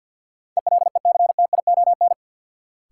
Morse code, heard as a keyed tone, about 52 wpm, 700 Hz. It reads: ELECTION
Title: Sentences from Top 1500 Words - New Word Drill - 50wpm